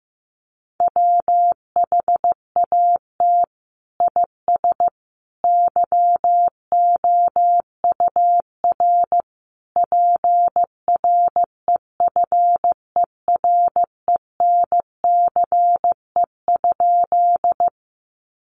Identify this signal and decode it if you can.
Morse code, heard as a keyed tone, about 15 wpm, 700 Hz. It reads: WHAT IS YOUR PREFERENCE?
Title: Sentences from Top 1500 Words 15wpm